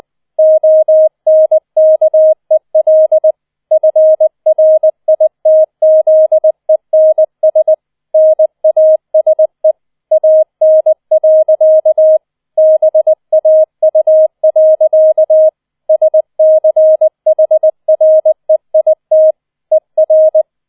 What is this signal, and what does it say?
Morse code, heard as a keyed tone, about 20 wpm, 630 Hz. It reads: ONKEL FRITZENS NASE AN. BAU. SCHREIT ER
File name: 20wpm